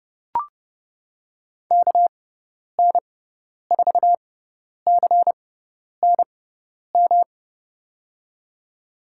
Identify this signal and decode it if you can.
Morse code, heard as a keyed tone, about 30 wpm, spaced as for 10 wpm, 700 Hz. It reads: KN4CNM
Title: USA Call Signs - 6 Characters - 6x Character Spacing - 30wpm